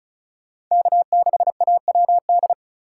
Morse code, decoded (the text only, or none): K6AWD